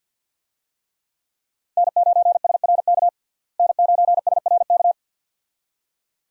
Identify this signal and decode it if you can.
Morse code, heard as a keyed tone, about 51 wpm, 700 Hz. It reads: N9SRK N9SRK
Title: USA Call Signs - 5 Characters - Repeated 2x - 50wpm